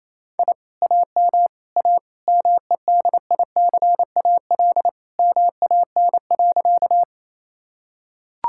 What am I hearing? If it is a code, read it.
Morse code, heard as a keyed tone, about 27 wpm, 700 Hz. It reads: I AM A MEDICAL MAN.
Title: Sentences from Top 1000 Words - Review Part 5 - 28wpm